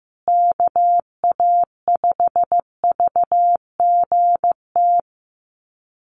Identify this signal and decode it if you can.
Morse code, heard as a keyed tone, about 15 wpm, 700 Hz. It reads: KA5VGT
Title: USA Call Signs Repeated 3x 15wpm